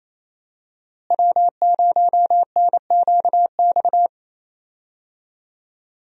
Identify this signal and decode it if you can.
Morse code, heard as a keyed tone, about 28 wpm, 700 Hz. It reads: W0NQX
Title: USA Call Signs - 5 Characters - Repeated 2x - 28wpm